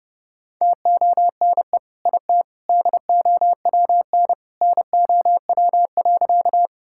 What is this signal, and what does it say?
Morse code, heard as a keyed tone, about 32 wpm, 700 Hz. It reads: TONE IT DOWN NOW.
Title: Sentences from Top 1600 Words 30wpm